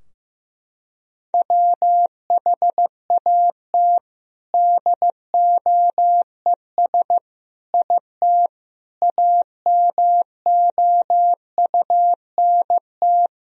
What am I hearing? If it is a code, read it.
Morse code, heard as a keyed tone, about 16 wpm, 700 Hz. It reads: WHAT DOES IT AMOUNT
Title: Sentences from Top 1000 Words - Review Part 4 - 15wpm